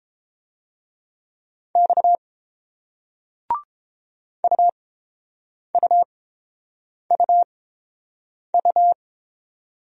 Morse code, heard as a keyed tone, about 30 wpm, spaced as for 7 wpm, 700 Hz. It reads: X UUUU